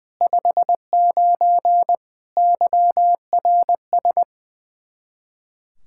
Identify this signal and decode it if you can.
Morse code, heard as a keyed tone, about 21 wpm, 700 Hz. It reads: 59 YRS
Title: Practice QSO - Exchange 3 - More Info - 20wpm